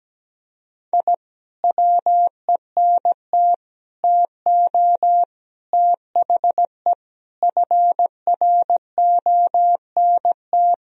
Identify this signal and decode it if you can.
Morse code, heard as a keyed tone, about 17 wpm, 700 Hz. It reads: I WENT TO THE FRONT
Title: Sentences from Top 1000 Words - Review Part 2 - 17wpm